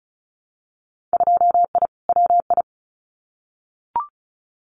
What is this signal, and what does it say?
Morse code, heard as a keyed tone, about 36 wpm, 700 Hz. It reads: 2I WI
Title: Winter Field Day Exchanges 35wpm